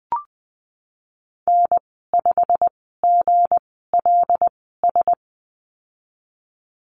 Morse code, heard as a keyed tone, about 19 wpm, spaced as for 14 wpm, 700 Hz. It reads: N5GLS
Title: 167 Lesson 27g - Number 7 - Call Signs 20wpm